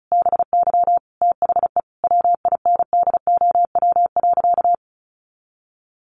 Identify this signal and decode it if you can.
Morse code, heard as a keyed tone, about 34 wpm, 700 Hz. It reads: BY THE WINDOW.
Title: Additional Easy Phrases 35wpm